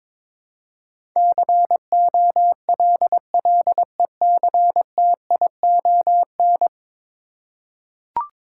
Morse code, heard as a keyed tone, about 22 wpm, 700 Hz. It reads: COLLECTION
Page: morse-code-ninja-practice-22wpm